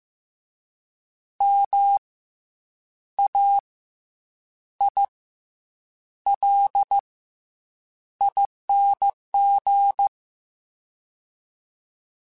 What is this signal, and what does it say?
Morse code, heard as a keyed tone, about 15 wpm, spaced as for 7 wpm, 800 Hz. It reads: MAILING
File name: Special INGs-mp3s / Special INGs-mp3s / Fast ing @13